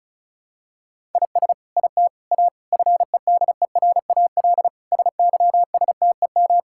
Morse code, heard as a keyed tone, about 36 wpm, 700 Hz. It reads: IS IT A FEDERAL SYSTEM
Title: Sentences from Top 1000 Words - Review Part 5 - 35wpm